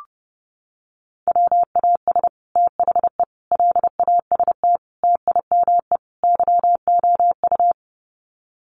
Morse code, heard as a keyed tone, about 30 wpm, 700 Hz. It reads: WAS THE LAST TIME YOU
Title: Sets of 5 Words Encore2 30wpm